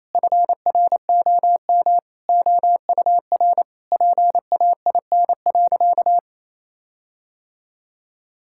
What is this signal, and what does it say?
Morse code, heard as a keyed tone, about 28 wpm, 700 Hz. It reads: FROM OUR PAIN.
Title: Sentences from Top 1600 Words 28wpm